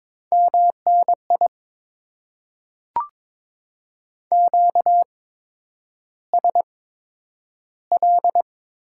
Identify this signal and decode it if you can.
Morse code, heard as a keyed tone, about 22 wpm, spaced as for 9 wpm, 700 Hz. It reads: MNI QSL
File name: Top 90 QSO Elements - 8x Character Spacing - 22wpm